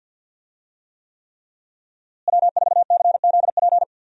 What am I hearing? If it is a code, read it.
Morse code, heard as a keyed tone, about 46 wpm, 700 Hz. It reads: W4CZP
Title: USA Call Signs - 5 Characters - Repeated 3x - 50wpm